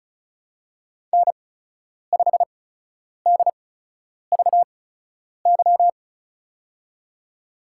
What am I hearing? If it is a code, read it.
Morse code, heard as a keyed tone, about 34 wpm, spaced as for 10 wpm, 700 Hz. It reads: N5DVY
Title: USA Call Signs - 5 Characters - 8x Character Spacing - 35wpm